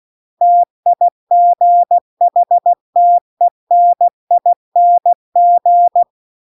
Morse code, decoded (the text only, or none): TIGHTENING